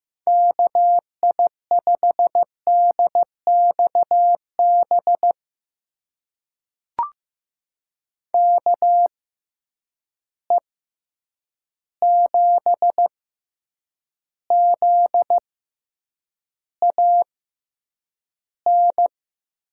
Morse code, heard as a keyed tone, about 15 wpm, spaced as for 8 wpm, 700 Hz. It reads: KI5DXB KE7ZAN
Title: USA Call Signs - 6 Characters - 6x Character Spacing - 15wpm